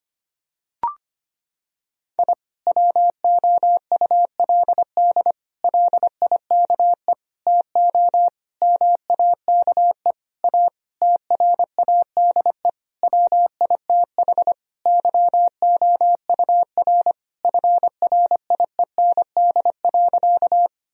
Morse code, heard as a keyed tone, about 25 wpm, 700 Hz. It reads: I WOULD LIKE TO MAKE A TRADE WITH YOUR FRIEND.
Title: Sentences from Top 1000 Words - Review Part 4 - 25wpm